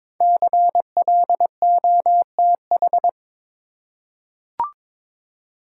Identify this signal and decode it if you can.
Morse code, heard as a keyed tone, about 23 wpm, 700 Hz. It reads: CLOTH
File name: 5 Letter Words - First 3 Letters Given - 22wpm